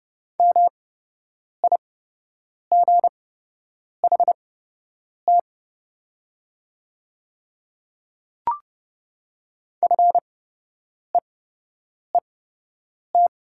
Morse code, heard as a keyed tone, about 30 wpm, spaced as for 8 wpm, 700 Hz. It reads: MIGHT FEET
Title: 8x Character Spacing - Top 300 Words - 30wpm